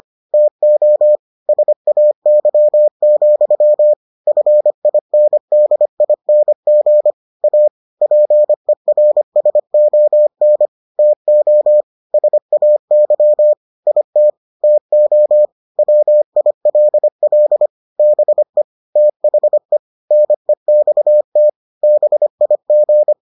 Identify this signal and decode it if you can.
Morse code, heard as a keyed tone, about 25 wpm, 600 Hz. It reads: TO SAY, FINDING A PERSON TO SAY IT TO WILL BE THE NEXT BIG